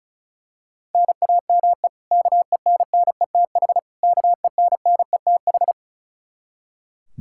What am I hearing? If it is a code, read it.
Morse code, heard as a keyed tone, about 34 wpm, 700 Hz. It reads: NAME KENNETH KENNETH
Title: Practice QSO - Exchange 1 - Beginning - 35wpm